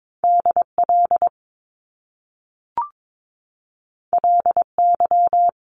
Morse code, heard as a keyed tone, about 22 wpm, 700 Hz. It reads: DL LY
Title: Top Two Letter Combinations in Words 22wpm